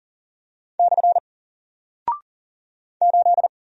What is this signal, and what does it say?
Morse code, heard as a keyed tone, about 40 wpm, 700 Hz. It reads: / 8
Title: Single Character (Letter:Number:Punctuation:Prosign) Alt 40wpm